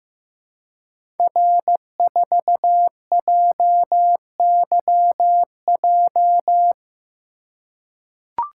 Morse code, heard as a keyed tone, about 15 wpm, 700 Hz. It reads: R4JYJ
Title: Code Groups - 5 Letter:Numbers - 15wpm